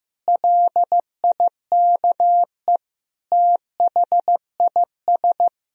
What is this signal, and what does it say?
Morse code, heard as a keyed tone, about 16 wpm, 700 Hz. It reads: LIKE THIS
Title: Sentences from Top 1000 Words - Review Part 4 - 15wpm